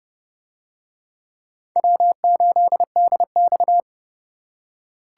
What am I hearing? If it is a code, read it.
Morse code, heard as a keyed tone, about 30 wpm, 700 Hz. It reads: W8DX